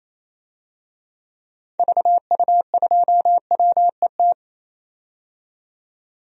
Morse code, heard as a keyed tone, about 28 wpm, 700 Hz. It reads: VU2WET
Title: DXCC-100 Call Signs - 6 Characters - 28wpm